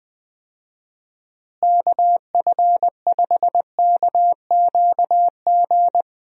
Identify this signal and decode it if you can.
Morse code, heard as a keyed tone, about 20 wpm, 700 Hz. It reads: KF5KQG
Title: USA Call Signs - 6 Characters - Repeated 2x - 20wpm